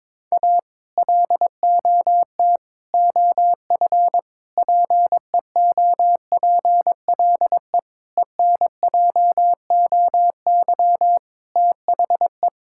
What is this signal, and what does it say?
Morse code, heard as a keyed tone, about 22 wpm, 700 Hz. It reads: A LOT OF PEOPLE ENJOY THE
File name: practice / 22wpm / Sentences with 4-Word Phrases - 22wpm